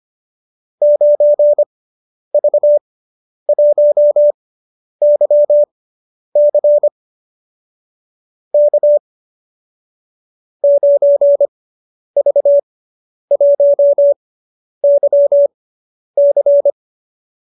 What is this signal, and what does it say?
Morse code, heard as a keyed tone, about 25 wpm, spaced as for 10 wpm, 590 Hz. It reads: 9V1YC K 9V1YC